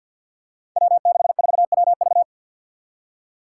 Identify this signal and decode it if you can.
Morse code, heard as a keyed tone, about 47 wpm, 700 Hz. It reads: W64WV
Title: Five Challenging Letter-Number Combinations 50wpm